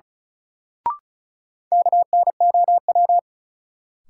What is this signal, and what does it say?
Morse code, heard as a keyed tone, about 36 wpm, 700 Hz. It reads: KNOW
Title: Sets of 2 Words - New Word Drill - 35wpm